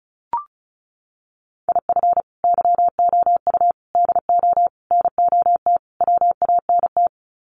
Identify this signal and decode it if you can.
Morse code, heard as a keyed tone, about 34 wpm, 700 Hz. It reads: IF YOU DO NOT WANT
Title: Sets of 5 Words 35wpm